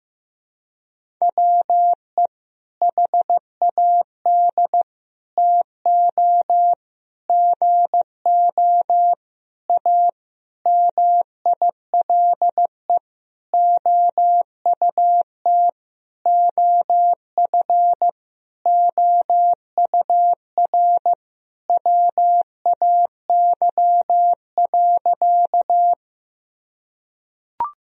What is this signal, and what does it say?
Morse code, heard as a keyed tone, about 15 wpm, 700 Hz. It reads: WE HAD TO GO A MILE OUT OF OUR WAY.
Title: Sentences from Top 1000 Words - Review Part 1 - 15wpm